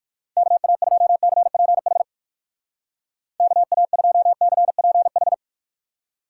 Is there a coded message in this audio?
Yes: Morse code from 0.4 to 5.3 s, heard as a keyed tone, about 57 wpm, 700 Hz. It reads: KA2CPH KA2CPH